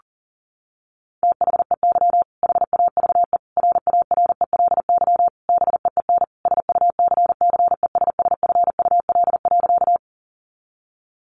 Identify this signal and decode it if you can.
Morse code, heard as a keyed tone, about 40 wpm, 700 Hz. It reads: THEY HAVE RARELY BEEN SUCCESSFUL.
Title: Sentences from Top 1400 Words 40wpm